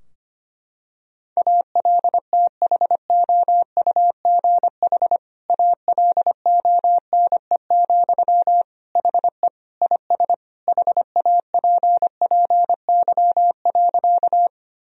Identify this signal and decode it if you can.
Morse code, heard as a keyed tone, about 25 wpm, 700 Hz. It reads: ALTHOUGH ALONE, HE IS HAPPY.
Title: Sentences from Top 1000 Words - Review Part 3 - 25wpm